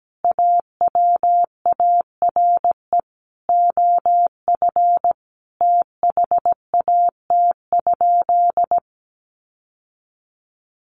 Morse code, heard as a keyed tone, about 17 wpm, 700 Hz. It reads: AWARE OF THAT?
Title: Sentences from Top 1200 Words 17wpm